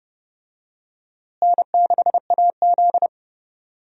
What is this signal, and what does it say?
Morse code, heard as a keyed tone, about 29 wpm, 700 Hz. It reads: N6AZ
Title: USA Call Signs - 4 Characters - Repeated 2x - 30wpm